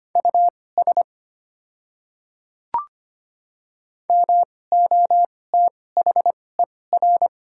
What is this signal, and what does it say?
Morse code, heard as a keyed tone, about 25 wpm, spaced as for 18 wpm, 690 Hz. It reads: US MOTHER
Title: 2x Character Spacing - Top 300 Words - 25wpm